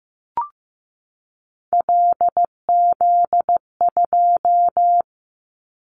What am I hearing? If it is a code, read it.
Morse code, heard as a keyed tone, about 15 wpm, 700 Hz. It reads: LZ2
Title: Callsign Prefixes for Top 50 DXCC - 15wpm